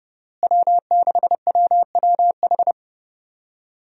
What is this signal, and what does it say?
Morse code, heard as a keyed tone, about 30 wpm, 700 Hz. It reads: W6WWH
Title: USA Call Signs - 5 Characters - Repeated 3x - 30wpm